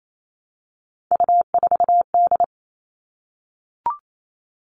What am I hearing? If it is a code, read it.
Morse code, heard as a keyed tone, about 28 wpm, 700 Hz. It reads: U4D
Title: All Cut Numbers - 3-Digit 28wpm